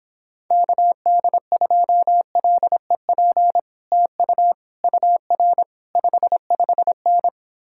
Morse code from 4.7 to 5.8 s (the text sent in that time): UR